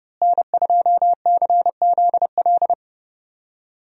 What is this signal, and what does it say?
Morse code, heard as a keyed tone, about 30 wpm, 700 Hz. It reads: N2CZL
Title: USA Call Signs - 5 Characters - Repeated 2x - 30wpm